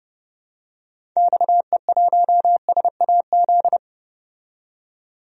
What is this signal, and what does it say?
Morse code, heard as a keyed tone, about 30 wpm, 700 Hz. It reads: XE1SAZ
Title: DXCC-100 Call Signs - 6 Characters - 30wpm